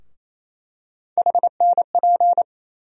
Morse code, heard as a keyed tone, about 28 wpm, 700 Hz. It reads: HNP
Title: USA Call Sign Postfixes - 3 Letters 28wpm